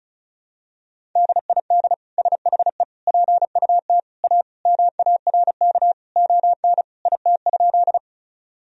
Morse code, heard as a keyed tone, about 34 wpm, 700 Hz. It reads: DID SHE PUT A MARK ON IT?
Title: Sentences from Top 1000 Words - Review Part 1 - 35wpm